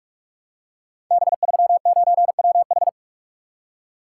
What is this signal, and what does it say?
Morse code, heard as a keyed tone, about 46 wpm, 700 Hz. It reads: B39WH
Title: Code Groups - 5 Letter:Numbers - 45wpm